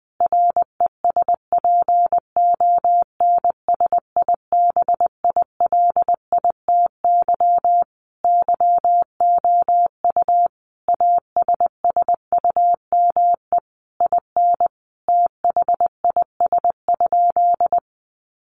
Morse code, heard as a keyed tone, about 20 wpm, 700 Hz. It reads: RESPONSIBILITY YOU ASSUME IN THIS?
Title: Sentences from Top 1500 Words 20wpm